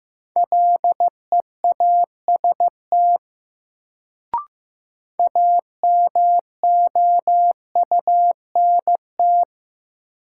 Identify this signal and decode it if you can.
Morse code, heard as a keyed tone, about 15 wpm, 700 Hz. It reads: LEAST AMOUNT
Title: Sentences from Top 700 Words - New Word Drill - 15wpm